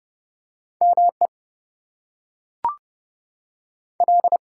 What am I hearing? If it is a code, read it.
Morse code, heard as a keyed tone, about 30 wpm, 700 Hz. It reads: ME L
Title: US States and Canadian Provinces 30wpm